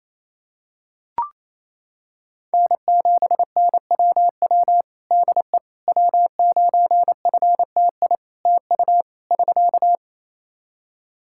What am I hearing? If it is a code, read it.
Morse code, heard as a keyed tone, about 28 wpm, 700 Hz. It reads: N7NWW DE W9FTI TU <SK>
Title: Practice QSO - Exchange 4 - Ending - 28wpm